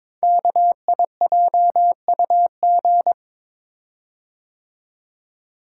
Morse code, heard as a keyed tone, about 22 wpm, 700 Hz. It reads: KIJUG
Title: Code Groups - 5 Letters - 22wpm